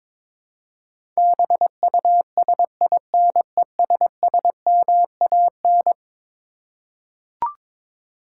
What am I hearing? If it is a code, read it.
Morse code, heard as a keyed tone, about 22 wpm, 700 Hz. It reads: BUSINESSMAN
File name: practice / 22wpm / Eleven Letter Words - 22wpm